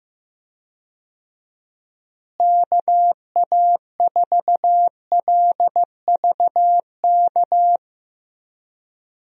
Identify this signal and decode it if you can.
Morse code, heard as a keyed tone, about 15 wpm, 700 Hz. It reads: KA4LVK
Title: USA Call Signs - 6 Characters - Repeated 3x - 15wpm